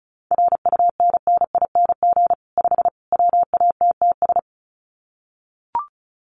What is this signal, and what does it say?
Morse code, heard as a keyed tone, about 34 wpm, 700 Hz. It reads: RUNNING 5 WATTS
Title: Common QSO Phrases 35wpm